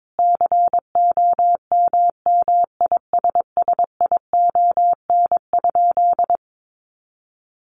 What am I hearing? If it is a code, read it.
Morse code, heard as a keyed tone, about 22 wpm, 700 Hz. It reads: COMMISSION?
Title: Sentences from Top 1400 Words 22wpm